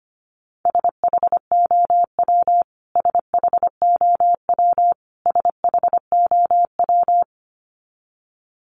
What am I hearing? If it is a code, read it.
Morse code, heard as a keyed tone, about 25 wpm, 700 Hz. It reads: SHOW SHOW SHOW
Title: Top 500 Words Repeated 25wpm